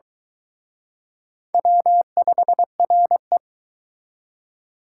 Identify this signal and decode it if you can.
Morse code, heard as a keyed tone, about 23 wpm, 700 Hz. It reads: W5RE